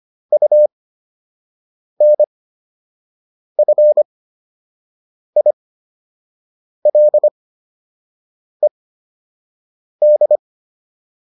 Morse code, heard as a keyed tone, about 25 wpm, spaced as for 6 wpm, 600 Hz. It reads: UNFILED